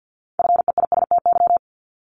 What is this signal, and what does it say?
Morse code, heard as a keyed tone, about 50 wpm, 700 Hz. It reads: FEISTY
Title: Multi-letter Phonemes 50wpm